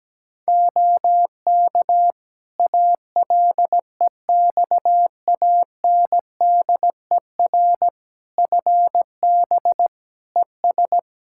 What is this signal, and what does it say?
Morse code, heard as a keyed tone, about 17 wpm, 700 Hz. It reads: OK ALEXANDER FB ES
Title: K1USN SST QSO Practice 17wpm